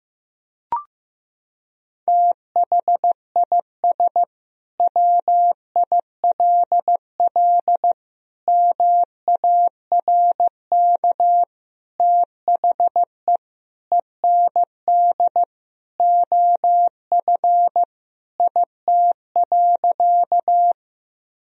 Morse code, heard as a keyed tone, about 15 wpm, 700 Hz. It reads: THIS WILL MARK THE END OF IT.
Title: Sentences from Top 1000 Words - Review Part 1 - 15wpm